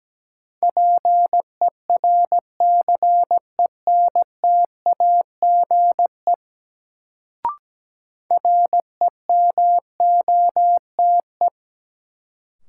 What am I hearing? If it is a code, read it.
Morse code, heard as a keyed tone, about 17 wpm, 700 Hz. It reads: PERCENTAGE REMOTE
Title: Sentences from Top 1300 Words - New Word Drill - 17wpm